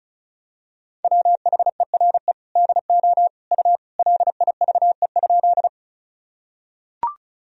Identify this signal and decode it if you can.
Morse code, heard as a keyed tone, about 36 wpm, 690 Hz. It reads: WHERE DO U LIVE?
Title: Sentences from Top 200 Words and Abbreviations 35wpm